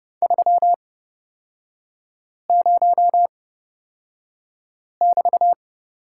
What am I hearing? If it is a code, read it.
Morse code, heard as a keyed tone, about 30 wpm, spaced as for 5 wpm, 700 Hz. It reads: 30<BT>
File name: Lesson 36e - BT - Three Characters 30wpm